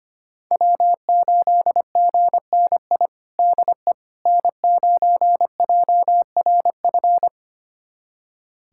Morse code, heard as a keyed tone, about 25 wpm, 700 Hz. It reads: W8GNI DE N9JRF
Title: USA Callsign DE Callsign - 5 Characters - 25wpm